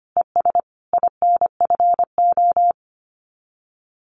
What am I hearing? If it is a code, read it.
Morse code, heard as a keyed tone, about 25 wpm, 700 Hz. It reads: ES INFO
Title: Common QSO Phrases Encore 25wpm